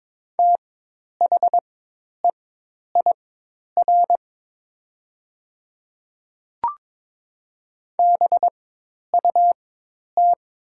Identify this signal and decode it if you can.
Morse code, heard as a keyed tone, about 22 wpm, spaced as for 10 wpm, 700 Hz. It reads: THEIR BUT
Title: 4x Character Spacing - Top 100 Words - 22wpm